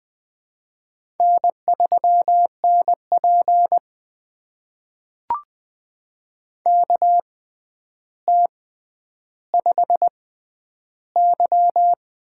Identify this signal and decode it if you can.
Morse code, heard as a keyed tone, about 20 wpm, spaced as for 10 wpm, 700 Hz. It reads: N3NP KT5Y
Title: USA Call Signs - 4 Characters - 6x Character Spacing - 20wpm